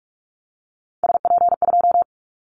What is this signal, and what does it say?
Morse code, heard as a keyed tone, about 44 wpm, 700 Hz. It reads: SP2
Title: Callsign Prefixes for Top 25 DXCC - 45wpm